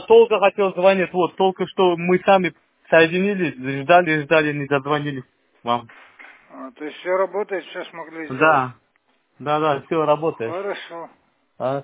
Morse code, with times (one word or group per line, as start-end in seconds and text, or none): none